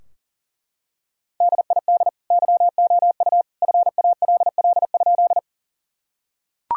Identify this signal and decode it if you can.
Morse code, heard as a keyed tone, about 37 wpm, 700 Hz. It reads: DID YOU FALL?
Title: Sentences from Top 1000 Words - Review Part 2 - 40wpm